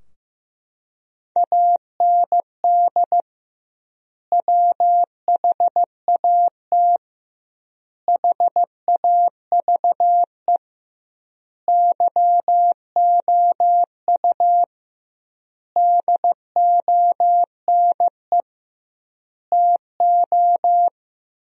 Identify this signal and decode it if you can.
Morse code, heard as a keyed tone, about 15 wpm, 700 Hz. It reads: AND WHAT HAVE YOU DONE TO
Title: 2x Word-Spacing - Sentences from Top 500 Words - 15wpm